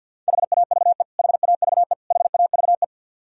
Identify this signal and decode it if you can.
Morse code, heard as a keyed tone, about 51 wpm, 700 Hz. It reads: HAVE HAVE HAVE
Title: Top 100 Words Repeated 50wpm